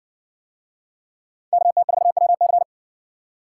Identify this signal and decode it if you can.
Morse code, heard as a keyed tone, about 58 wpm, 700 Hz. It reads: KT4WC